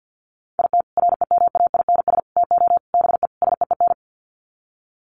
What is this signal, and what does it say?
Morse code, heard as a keyed tone, about 50 wpm, 700 Hz. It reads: IT REMAINS TO BE SEEN
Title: Sets of 5 Words Encore 50wpm